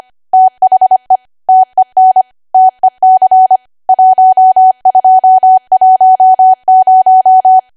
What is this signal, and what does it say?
Morse code, heard as a keyed tone, about 25 wpm, 750 Hz. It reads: THE TEN TEC 1210
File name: Archive / 25 WPM / 25WPM